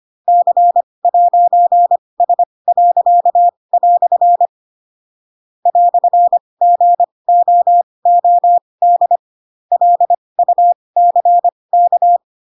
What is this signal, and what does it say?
Morse code, heard as a keyed tone, about 25 wpm, spaced as for 20 wpm, 700 Hz. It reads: C'S." "GOOD LUCK
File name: Pinocchio / Pinocchio ch2 15 25 ik1hge